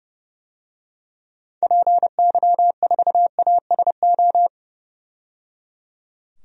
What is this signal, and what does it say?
Morse code, heard as a keyed tone, about 30 wpm, 700 Hz. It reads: PY4ASO
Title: DXCC-100 Call Signs - 6 Characters - 30wpm